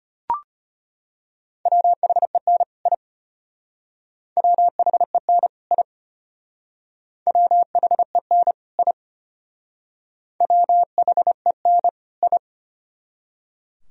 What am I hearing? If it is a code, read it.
Morse code, heard as a keyed tone, about 33 wpm, 700 Hz. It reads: WHEN I WHEN I WHEN I WHEN I